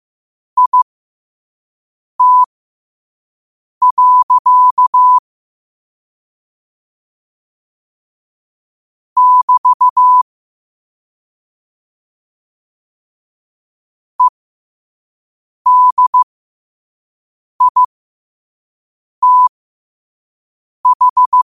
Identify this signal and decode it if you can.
Morse code, heard as a keyed tone, about 15 wpm, spaced as for 5 wpm, 1000 Hz. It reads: IT. <BT> EDITH